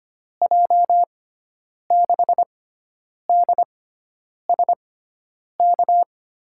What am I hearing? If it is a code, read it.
Morse code, heard as a keyed tone, about 25 wpm, spaced as for 9 wpm, 700 Hz. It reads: J6DSK
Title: USA Call Signs - 6 Characters - 6x Character Spacing - 25wpm